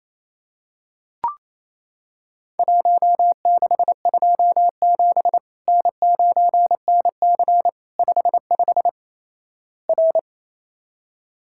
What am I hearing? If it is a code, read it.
Morse code, heard as a keyed tone, about 28 wpm, 700 Hz. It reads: 1627 N9NC 55 R
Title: Worked All Europe DX Contest - QTC Exchange - 28wpm